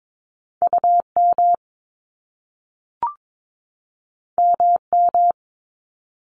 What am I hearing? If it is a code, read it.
Morse code, heard as a keyed tone, about 22 wpm, 700 Hz. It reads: UM MM
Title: Top Two Letter Combinations in Words 22wpm